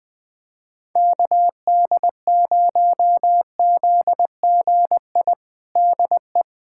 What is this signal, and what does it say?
Morse code, heard as a keyed tone, about 21 wpm, 700 Hz. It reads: KD0ZGI DE
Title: USA Callsign DE Callsign - 6 Characters - 20wpm